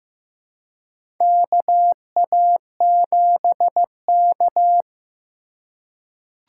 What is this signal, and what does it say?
Morse code, heard as a keyed tone, about 15 wpm, 700 Hz. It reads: KA7K
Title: USA Call Signs - 4 Characters - 4x Character Spacing - 15wpm